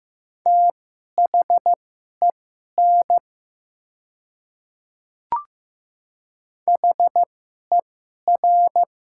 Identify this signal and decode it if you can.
Morse code, heard as a keyed tone, about 15 wpm, spaced as for 11 wpm, 700 Hz. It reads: THEN HER
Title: 2x Character Spacing - Top 100 Words - 15wpm